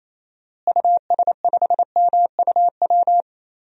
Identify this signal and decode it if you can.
Morse code, heard as a keyed tone, about 28 wpm, 700 Hz. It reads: US5MUW
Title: DXCC-100 Call Signs - 6 Characters - Repeated 3x - 28wpm